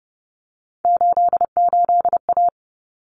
Morse code, heard as a keyed tone, about 30 wpm, 700 Hz. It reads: 88A
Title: Common Cut Numbers - 3-Digit 30wpm